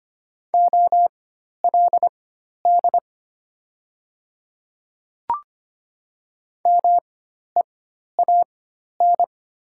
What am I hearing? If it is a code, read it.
Morse code, heard as a keyed tone, about 25 wpm, spaced as for 12 wpm, 700 Hz. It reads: OLD MEAN
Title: 4x Character Spacing - Top 300 Words - 25wpm